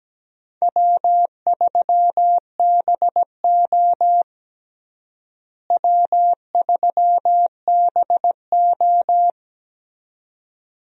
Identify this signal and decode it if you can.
Morse code, heard as a keyed tone, about 17 wpm, 700 Hz. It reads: W3BO W3BO
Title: USA Call Signs - 4 Characters - Repeated 2x - 17wpm